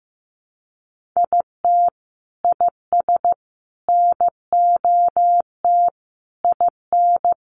Morse code, heard as a keyed tone, about 15 wpm, 700 Hz. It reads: IT IS NOT IN
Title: Sentences from Top 1200 Words 15wpm